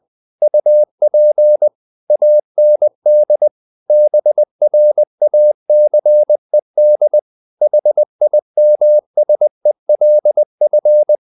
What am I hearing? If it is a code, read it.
Morse code, heard as a keyed tone, about 20 wpm, 600 Hz. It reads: UP AND BRACED HIMSELF
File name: the old man and the sea / the old man and the sea part 1 20wpm